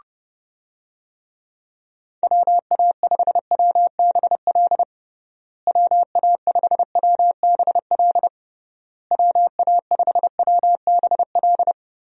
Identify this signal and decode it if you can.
Morse code, heard as a keyed tone, about 30 wpm, 700 Hz. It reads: WA5WBL WA5WBL WA5WBL
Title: USA Call Signs - 6 Characters - Repeated 3x - 30wpm